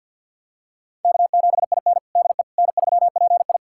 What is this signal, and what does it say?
Morse code, heard as a keyed tone, about 49 wpm, 690 Hz. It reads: K7IN DE N3PI
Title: USA Callsign DE Callsign - 4 Characters - 50wpm